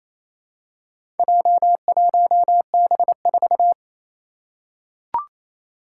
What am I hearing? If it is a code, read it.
Morse code, heard as a keyed tone, about 28 wpm, 700 Hz. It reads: J1B4
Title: Four Challenging Letter-Number Combinations 28wpm